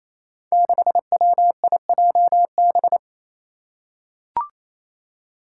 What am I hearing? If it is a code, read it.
Morse code, heard as a keyed tone, about 28 wpm, 700 Hz. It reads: 6WIJB
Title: Five Challenging Letter-Number Combinations 28wpm